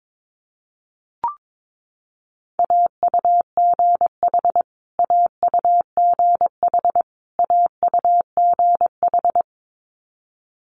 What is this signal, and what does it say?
Morse code, heard as a keyed tone, about 22 wpm, 700 Hz. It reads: AUGH AUGH AUGH
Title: Multi-letter Phonemes Repeated 3x 22wpm